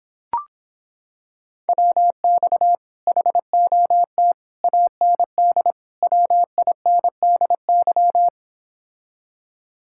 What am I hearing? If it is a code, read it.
Morse code, heard as a keyed tone, about 26 wpm, 700 Hz. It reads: WX HOT AND WINDY